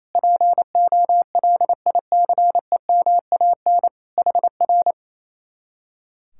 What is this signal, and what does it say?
Morse code, heard as a keyed tone, about 26 wpm, 700 Hz. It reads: POLICEMAN HR
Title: Speed-Racing Common QSO Phrases - 28wpm